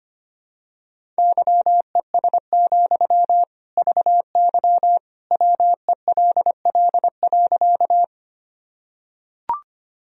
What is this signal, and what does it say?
Morse code, heard as a keyed tone, about 25 wpm, 700 Hz. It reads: YES, VY WELL.
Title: Sentences from Top 500 Words and Abbreviations 25wpm